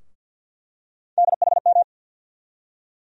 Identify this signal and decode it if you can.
Morse code, heard as a keyed tone, about 46 wpm, 700 Hz. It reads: DHM
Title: USA Call Sign Postfixes - 3 Letters 50wpm